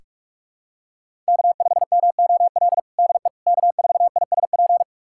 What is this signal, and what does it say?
Morse code, heard as a keyed tone, about 41 wpm, 700 Hz. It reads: K5MOL DE K4ISP
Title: USA Callsign DE Callsign - 5 Characters - 45wpm